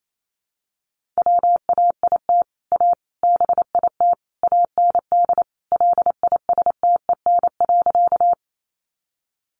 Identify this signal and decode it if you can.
Morse code, heard as a keyed tone, about 28 wpm, 700 Hz. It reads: WAIT A BIT AND LISTEN.
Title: Sentences from Top 900 Words 28wpm